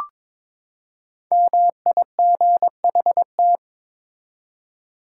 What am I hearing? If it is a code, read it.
Morse code, heard as a keyed tone, about 22 wpm, 700 Hz. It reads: MIGHT
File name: Build A Phrase - 2-Word Phrases - 22wpm